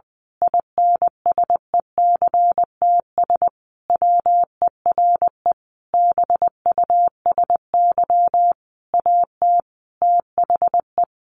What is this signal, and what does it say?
Morse code, heard as a keyed tone, about 20 wpm, 700 Hz. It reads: INSECTS WERE BUSY AT THE
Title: A Sound Of Thunder 20wpm